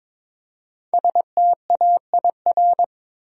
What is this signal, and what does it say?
Morse code, heard as a keyed tone, about 22 wpm, 700 Hz. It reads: STAIR
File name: 5 Letter Words - First 3 Letters Given - 22wpm